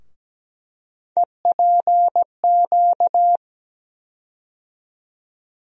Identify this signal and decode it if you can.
Morse code, heard as a keyed tone, about 17 wpm, 700 Hz. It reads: EPQ
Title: USA Call Sign Postfixes - 3 Letters 17wpm